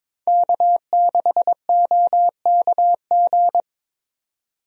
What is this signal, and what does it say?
Morse code, heard as a keyed tone, about 22 wpm, 700 Hz. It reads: K6OKG
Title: USA Call Signs - 5 Characters - 22wpm